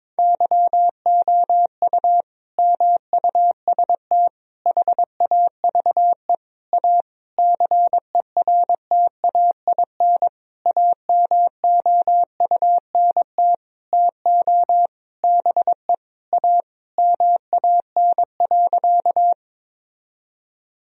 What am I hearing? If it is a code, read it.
Morse code, heard as a keyed tone, about 22 wpm, 700 Hz. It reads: YOU MUST HAVE A CERTAIN AMOUNT TO BE A MAN.
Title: Sentences from Top 1000 Words - Review Part 4 - 22wpm